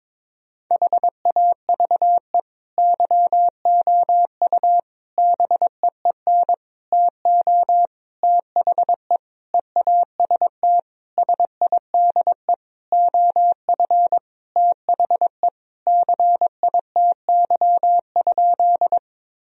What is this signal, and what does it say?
Morse code, heard as a keyed tone, about 22 wpm, 700 Hz. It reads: HAVE YOU BEEN TO THE EAST SIDE OF THE CITY?
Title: Sentences from Top 1000 Words - Review Part 3 - 22wpm